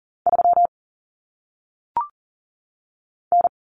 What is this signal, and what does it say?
Morse code, heard as a keyed tone, about 40 wpm, 700 Hz. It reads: 3 N
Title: Common Cut Numbers - 1-Digit 40wpm